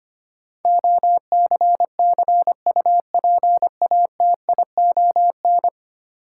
Morse code, heard as a keyed tone, about 25 wpm, 700 Hz. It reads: OCCUPATION
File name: Ten Letter Words - 25wpm